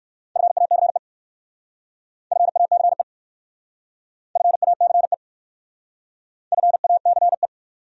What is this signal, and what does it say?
Morse code, heard as a keyed tone, about 58 wpm, 700 Hz. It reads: FACE FACE FACE FACE